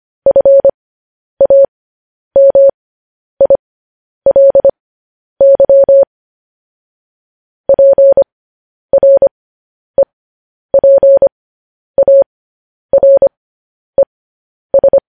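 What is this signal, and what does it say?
Morse code, heard as a keyed tone, about 25 wpm, spaced as for 10 wpm, 550 Hz. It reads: FAMILY PREPARES
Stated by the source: Morse code transmission of news headlines.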